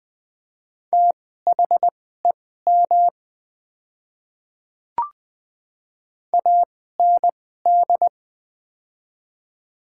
Morse code, heard as a keyed tone, about 20 wpm, spaced as for 15 wpm, 700 Hz. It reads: THEM AND
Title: 2x Character Spacing - Top 200 Words - 20wpm